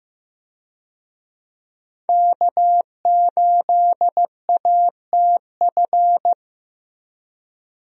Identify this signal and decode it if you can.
Morse code, heard as a keyed tone, about 15 wpm, 700 Hz. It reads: K8ATF
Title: USA Call Signs - 5 Characters - Repeated 2x - 15wpm